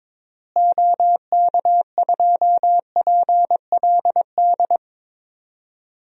Morse code, heard as a keyed tone, about 22 wpm, 700 Hz. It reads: OK2PLD
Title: DXCC-100 Call Signs - 6 Characters - Repeated 3x - 22wpm